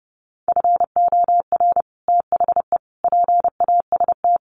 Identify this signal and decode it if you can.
Morse code, heard as a keyed tone, about 30 wpm, 700 Hz. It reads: FOR THE PAST
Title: Sets of 5 Words Encore2 30wpm